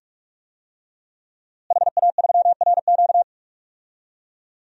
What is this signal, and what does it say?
Morse code, heard as a keyed tone, about 47 wpm, 700 Hz. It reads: HA3RQ
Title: DXCC-100 Call Signs - 5 Characters - Repeated 2x - 45wpm